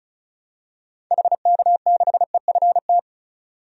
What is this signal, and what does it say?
Morse code, heard as a keyed tone, about 36 wpm, 700 Hz. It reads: HK6EFT
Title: DXCC-100 Call Signs - 6 Characters - 35wpm